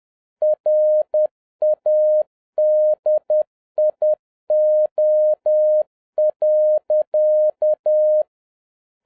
Morse code, heard as a keyed tone, about 10 wpm, 610 Hz. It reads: RADIO.